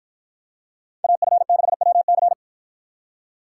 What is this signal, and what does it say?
Morse code, heard as a keyed tone, about 53 wpm, 700 Hz. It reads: AF6WC